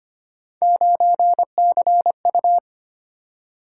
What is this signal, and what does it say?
Morse code, heard as a keyed tone, about 25 wpm, 700 Hz. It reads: 9CU
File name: Three Letter-Number Combinations - 25wpm